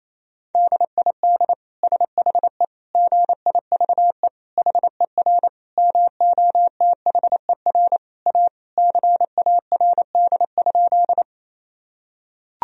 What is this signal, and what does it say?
Morse code, heard as a keyed tone, about 28 wpm, 700 Hz. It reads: DID SHE GIVE HER MOTHER A CARD?
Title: Sentences from Top 1000 Words - Review Part 4 - 28wpm